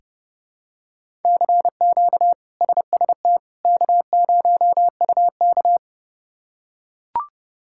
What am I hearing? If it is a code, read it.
Morse code, heard as a keyed tone, about 29 wpm, 700 Hz. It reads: CQ SST K0UK
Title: K1USN SST QSO Practice - 30wpm